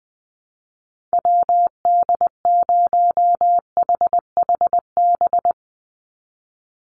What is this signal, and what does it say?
Morse code, heard as a keyed tone, about 20 wpm, 700 Hz. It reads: WD0HHB
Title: USA Call Signs 20wpm